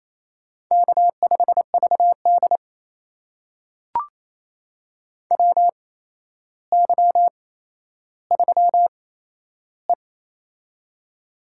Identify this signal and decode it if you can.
Morse code, heard as a keyed tone, about 27 wpm, spaced as for 12 wpm, 700 Hz. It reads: K5VD WY3E
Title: USA Call Signs - 4 Characters - 8x Character Spacing - 28wpm